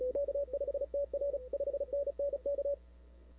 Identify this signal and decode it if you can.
Morse code, heard as a keyed tone, about 34 wpm, 560 Hz. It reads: K5TR 5NN K
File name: CW / 2005-k5tr arrl cw dx -00:00